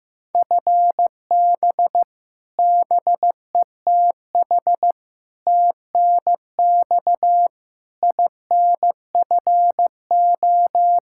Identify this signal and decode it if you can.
Morse code, heard as a keyed tone, about 16 wpm, 700 Hz. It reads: FB BETH TNX INFO
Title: Practice QSO - Exchange 3 - More Info - 15wpm